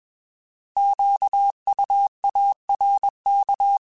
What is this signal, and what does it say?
Morse code, heard as a keyed tone, about 21 wpm, 800 Hz. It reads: QUARK
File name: Alphabetical / Alphabetical mp3s / Qs @20